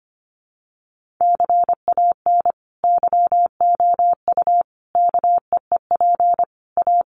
Morse code, heard as a keyed tone, about 25 wpm, 700 Hz. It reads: CAN YOU KEEP A
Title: Sentences from Top 1200 Words 25wpm